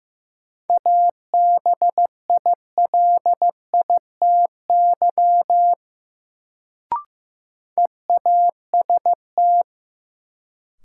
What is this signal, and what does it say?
Morse code, heard as a keyed tone, about 15 wpm, 700 Hz. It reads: ABILITY EAST
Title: Sentences from Top 1000 Words - Review Part 3 - New Word Drill - 15wpm